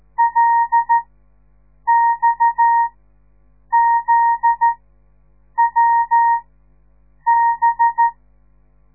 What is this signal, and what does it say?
Morse code, heard as a keyed tone, about 13 wpm, spaced as for 7 wpm, 930 Hz. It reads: LXZWB
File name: radiomorningstar / morse / les 9